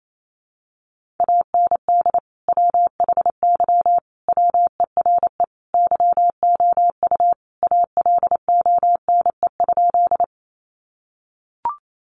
Morse code, heard as a keyed tone, about 28 wpm, 700 Hz. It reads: AND WHY WERE YOU ALONE?
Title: Sentences from Top 1000 Words 28wpm